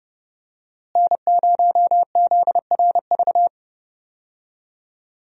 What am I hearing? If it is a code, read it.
Morse code, heard as a keyed tone, about 30 wpm, 700 Hz. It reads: N0ZRV
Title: USA Call Signs - 5 Characters - Repeated 3x - 30wpm